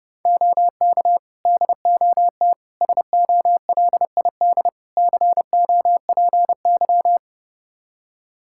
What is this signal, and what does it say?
Morse code, heard as a keyed tone, about 32 wpm, 700 Hz. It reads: OK DOT SOLID COPY
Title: Practice QSO - Exchange 3 - More Info - 30wpm